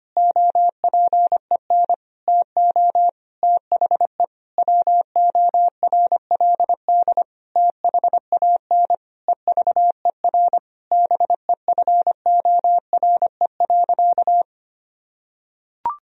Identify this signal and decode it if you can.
Morse code, heard as a keyed tone, about 25 wpm, 700 Hz. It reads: OPEN TO THE WORLD THAN EVER BEFORE.
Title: Sentences from Top 1000 Words - Review Part 3 - 25wpm